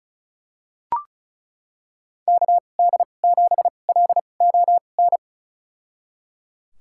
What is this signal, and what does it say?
Morse code, heard as a keyed tone, about 36 wpm, spaced as for 25 wpm, 700 Hz. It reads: KD7LON
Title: USA Call Signs - 6 Characters - 2x Character Spacing - 35wpm